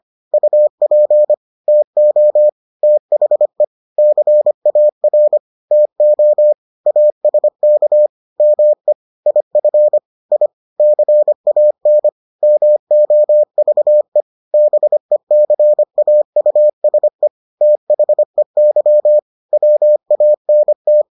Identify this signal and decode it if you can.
Morse code, heard as a keyed tone, about 25 wpm, 600 Hz. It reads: UP TO THE CAR TO ASK ME IF I CAN MOVE BECAUSE THEY WANT